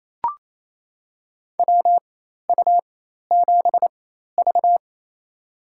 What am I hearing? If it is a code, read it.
Morse code, heard as a keyed tone, about 27 wpm, spaced as for 13 wpm, 700 Hz. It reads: WU7V
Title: USA Call Signs - 4 Characters - 4x Character Spacing - 28wpm